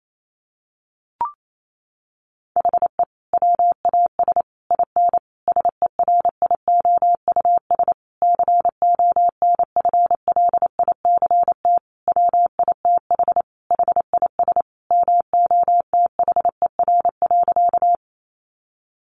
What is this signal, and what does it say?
Morse code, heard as a keyed tone, about 28 wpm, 700 Hz. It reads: HE WAS IN SERIOUS CONFLICT WITH HIS MOTHER.
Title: Sentences from Top 1300 Words 28wpm